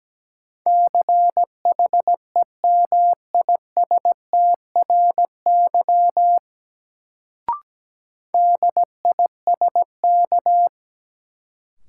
Morse code, heard as a keyed tone, about 17 wpm, 700 Hz. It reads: CHEMISTRY DISK
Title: Sentences from Top 1000 Words - Review Part 6 - New Word Drill - 17wpm